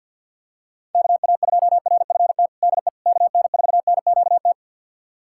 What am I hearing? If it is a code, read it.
Morse code, heard as a keyed tone, about 46 wpm, 700 Hz. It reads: KA2RFT DE KN4NQT
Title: USA Callsign DE Callsign 50wpm